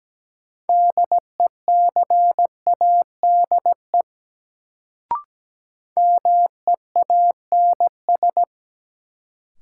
Morse code, heard as a keyed tone, about 17 wpm, 700 Hz. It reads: DECADE MEANS
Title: Sets of 5 Words - New Word Drill - 17wpm